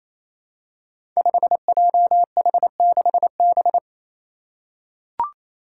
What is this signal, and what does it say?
Morse code, heard as a keyed tone, about 27 wpm, 700 Hz. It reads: 5JH6B
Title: Five Challenging Letter-Number Combinations 28wpm